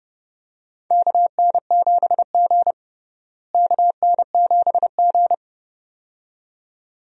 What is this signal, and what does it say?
Morse code, heard as a keyed tone, about 30 wpm, 700 Hz. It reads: KN7G KN7G
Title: USA Call Signs - 4 Characters - Repeated 2x - 30wpm